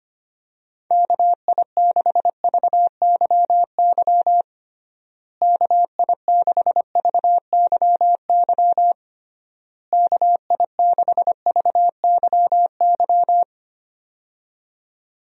Morse code, heard as a keyed tone, about 25 wpm, 700 Hz. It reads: KI6VYY KI6VYY KI6VYY
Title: USA Call Signs - 6 Characters - Repeated 3x - 25wpm